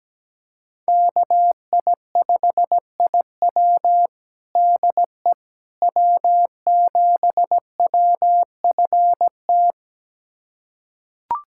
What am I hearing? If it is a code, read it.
Morse code, heard as a keyed tone, about 17 wpm, 700 Hz. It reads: KI5IW DE W7WFT
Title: USA Callsign DE Callsign 17wpm